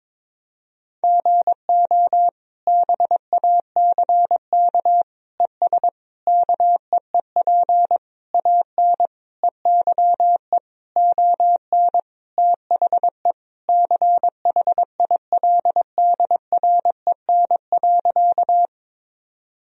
Morse code, heard as a keyed tone, about 22 wpm, 700 Hz. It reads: GO BACK ES KEEP AN EYE ON THE CHILDREN.
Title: Sentences from Top 300 Words and Abbreviations 22wpm